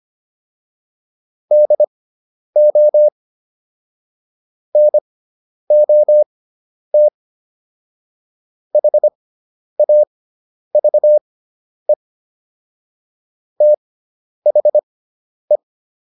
Morse code, heard as a keyed tone, about 25 wpm, spaced as for 10 wpm, 600 Hz. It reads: DO NOT HAVE THE